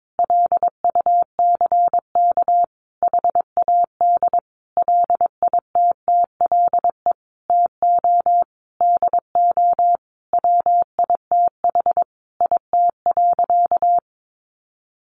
Morse code, heard as a keyed tone, about 22 wpm, 700 Hz. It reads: LUCK HAD LITTLE TO DO WITH IT.
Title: Sentences from Top 1500 Words 22wpm